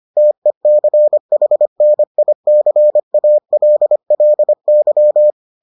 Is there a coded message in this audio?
Yes: Morse code from 0.2 to 5.3 s, heard as a keyed tone, about 25 wpm, 600 Hz. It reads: TECHNICALLY